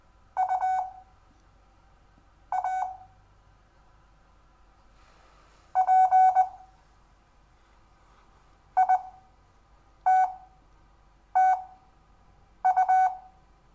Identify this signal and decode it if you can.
Morse code, heard as a keyed tone, about 20 wpm, spaced as for 7 wpm, 750 Hz. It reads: U A P ITTU
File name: cw / Lesson6 WI